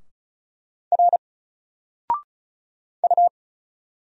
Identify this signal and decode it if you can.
Morse code, heard as a keyed tone, about 36 wpm, 700 Hz. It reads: R U
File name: Single-Letter-Number-35wpm